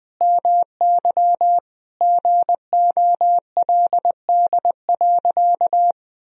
Morse code, heard as a keyed tone, about 20 wpm, 700 Hz. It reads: MY GOLD.
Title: Sentences from Top 1000 Words - Review Part 3 - 20wpm